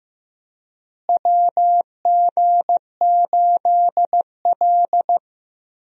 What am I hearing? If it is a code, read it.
Morse code, heard as a keyed tone, about 15 wpm, 700 Hz. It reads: WG8L
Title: USA Call Signs - 4 Characters - Repeated 3x - 15wpm